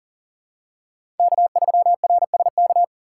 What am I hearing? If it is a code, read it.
Morse code, heard as a keyed tone, about 40 wpm, 700 Hz. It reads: K3RSK
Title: USA Call Signs - 5 Characters - 2x Character Spacing - 40wpm